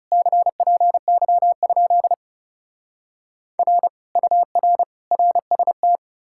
Morse code, heard as a keyed tone, about 40 wpm, 700 Hz. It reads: CPY? R UR RST